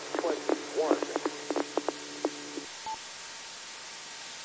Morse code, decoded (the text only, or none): none